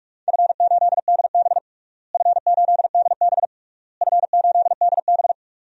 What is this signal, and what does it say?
Morse code, heard as a keyed tone, about 46 wpm, 700 Hz. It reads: F8DB F8DB F8DB
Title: DXCC-100 Call Signs - 4 Characters - Repeated 3x - 45wpm